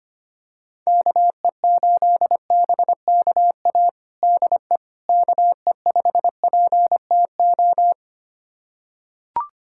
Morse code, heard as a keyed tone, about 25 wpm, 700 Hz. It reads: KE8BKA DE KE5PTO
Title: USA Callsign DE Callsign - 6 Characters - 25wpm